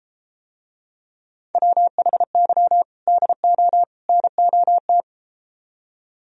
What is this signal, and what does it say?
Morse code, heard as a keyed tone, about 33 wpm, 700 Hz. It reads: WHY DO NOT